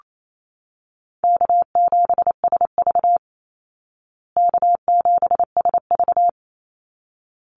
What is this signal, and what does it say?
Morse code, heard as a keyed tone, about 28 wpm, 700 Hz. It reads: K7SV K7SV
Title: Easy Call Signs Repeated 2x - 28wpm